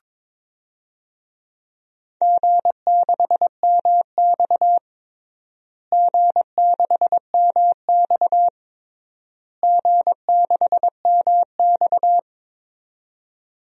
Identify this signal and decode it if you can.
Morse code, heard as a keyed tone, about 22 wpm, 700 Hz. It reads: G6MX G6MX G6MX
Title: DXCC-100 Call Signs - 4 Characters - Repeated 3x - 22wpm